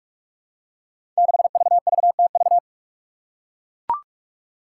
Morse code, heard as a keyed tone, about 42 wpm, 700 Hz. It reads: 6VVTV
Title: Five Challenging Letter-Number Combinations 45wpm